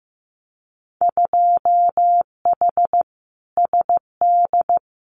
Morse code, heard as a keyed tone, about 15 wpm, 700 Hz. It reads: 2H SD
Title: Winter Field Day Exchanges 15wpm